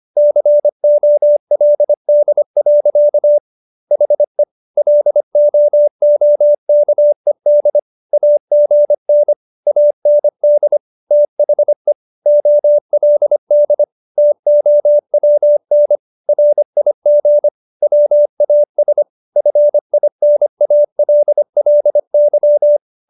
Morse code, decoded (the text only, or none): COLD. HE LOOKED AGN AND THE OLD TOWN RIG WAS FINALLY